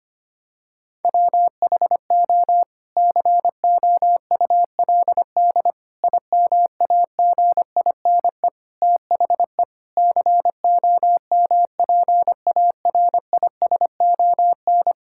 Morse code, heard as a keyed tone, about 25 wpm, 700 Hz. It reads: WHO COULD IMAGINE THE COMPARISON
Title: Sentences from Top 1600 Words 25wpm